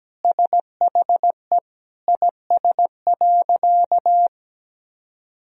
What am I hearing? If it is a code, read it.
Morse code, heard as a keyed tone, about 17 wpm, 700 Hz. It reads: SHE IS.
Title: Sentences from Top 1000 Words - Review Part 4 - 17wpm